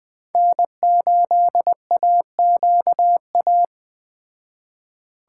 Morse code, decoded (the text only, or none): N8AQA